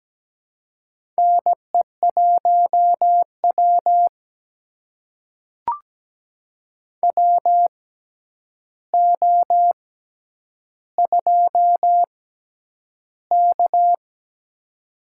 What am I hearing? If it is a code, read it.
Morse code, heard as a keyed tone, about 17 wpm, spaced as for 9 wpm, 700 Hz. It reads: NE1W WO2K
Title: USA Call Signs - 4 Characters - 6x Character Spacing - 17wpm